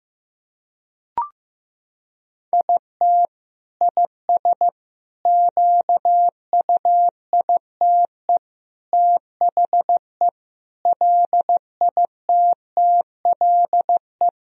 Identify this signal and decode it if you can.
Morse code, heard as a keyed tone, about 14 wpm, 700 Hz. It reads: IT IS QUITE THE LITTLE
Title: Sentences from Top 1000 Words - Review Part 2 - 15wpm